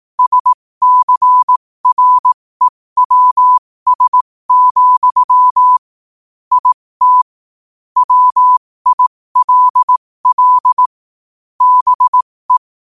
Morse code, heard as a keyed tone, about 18 wpm, spaced as for 15 wpm, 1000 Hz. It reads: SCREWS, IT WILL BE
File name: qotdfiles / 15wpm